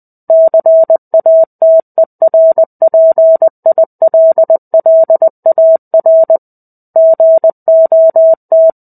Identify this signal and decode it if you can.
Morse code, heard as a keyed tone, about 20 wpm, 650 Hz. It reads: CATERPILLAR GOT
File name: 05-Chapter-05 20 WPM